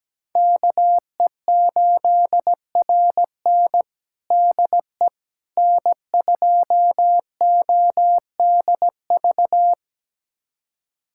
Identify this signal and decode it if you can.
Morse code, heard as a keyed tone, about 17 wpm, 700 Hz. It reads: KE8RN DE N2ODV
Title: USA Callsign DE Callsign - 5 Characters - 17wpm